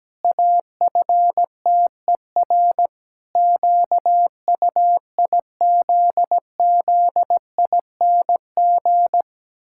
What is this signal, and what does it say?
Morse code, heard as a keyed tone, about 17 wpm, 700 Hz. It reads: AFTER QUIZZING
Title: Pangram Challenge v2 - 17wpm